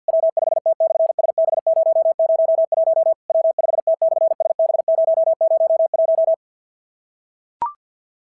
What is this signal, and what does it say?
Morse code, heard as a keyed tone, about 46 wpm, 660 Hz. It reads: W5T/SB001 W5T/SB001
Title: SOTA QSO Practice 50wpm